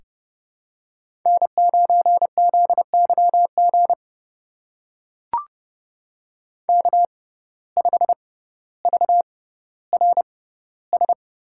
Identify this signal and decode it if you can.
Morse code, heard as a keyed tone, about 30 wpm, spaced as for 15 wpm, 700 Hz. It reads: N9ZYG K5VRS
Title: USA Call Signs - 5 Characters - 6x Character Spacing - 30wpm